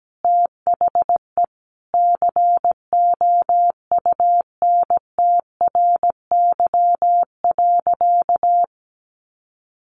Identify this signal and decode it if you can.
Morse code, heard as a keyed tone, about 17 wpm, 700 Hz. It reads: THE COUNTRY.
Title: Sentences from Top 700 Words 17wpm